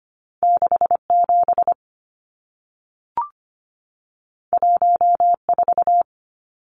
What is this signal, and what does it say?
Morse code, heard as a keyed tone, about 25 wpm, 700 Hz. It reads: 67 14
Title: Numbers - 2-Digit 25wpm